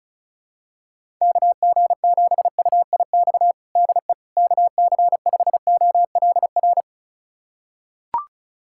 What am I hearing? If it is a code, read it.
Morse code, heard as a keyed tone, about 36 wpm, 690 Hz. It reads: KG7UIX DE KC5OLR
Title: USA Callsign DE Callsign - 6 Characters - 35wpm